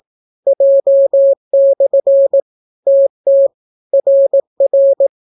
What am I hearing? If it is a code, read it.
Morse code, heard as a keyed tone, about 18 wpm, 550 Hz. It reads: J/ TT RR